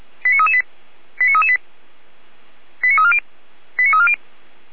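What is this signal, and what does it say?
Five-tone selective call (ZVEI), heard as quick runs of five tones, tones of about 80 ms. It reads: 89338 89338 89340 89340
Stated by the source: ZVEI five-tone call received by SDR, fire brigade Orlamünde (feuerwehr-orlamuende).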